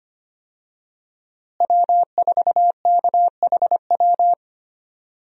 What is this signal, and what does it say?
Morse code, heard as a keyed tone, about 25 wpm, 700 Hz. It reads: W4KHW
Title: USA Call Signs - 5 Characters - Repeated 2x - 25wpm